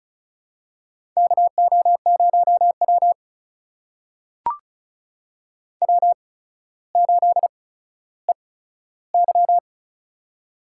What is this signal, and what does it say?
Morse code, heard as a keyed tone, about 36 wpm, spaced as for 15 wpm, 700 Hz. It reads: KO0W W8EY
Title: USA Call Signs - 4 Characters - 8x Character Spacing - 35wpm